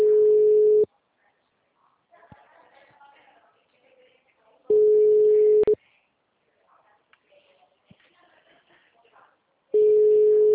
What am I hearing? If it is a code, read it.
Ringback tone (Europe, 425 Hz), with speech.
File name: out- -2014- - - .22179